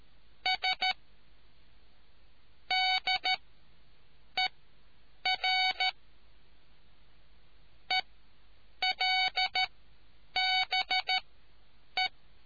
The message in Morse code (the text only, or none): S DER ELBE